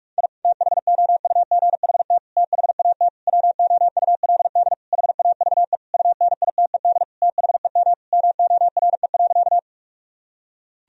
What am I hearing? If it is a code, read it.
Morse code, heard as a keyed tone, about 41 wpm, 700 Hz. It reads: I THOUGHT THAT WOULD HAVE UNITED THEM MORE.
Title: Sentences from Top 1000 Words - Review Part 4 - 45wpm